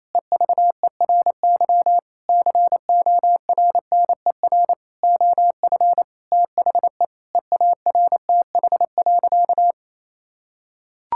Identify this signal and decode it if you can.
Morse code, heard as a keyed tone, about 28 wpm, 700 Hz. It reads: EVERY CORNER OF THE EARTH.
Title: Sentences from Top 1600 Words 28wpm